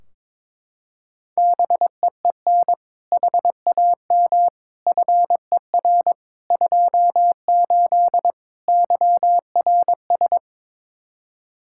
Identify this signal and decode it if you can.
Morse code, heard as a keyed tone, about 22 wpm, 700 Hz. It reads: BEEN HAM FER 28 YRS
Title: K1USN SST QSO Practice 22wpm